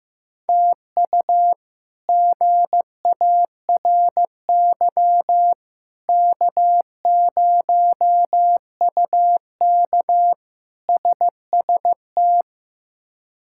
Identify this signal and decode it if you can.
Morse code, heard as a keyed tone, about 15 wpm, 700 Hz. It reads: TU GARY K0UK SST
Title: K1USN SST QSO Practice - 15wpm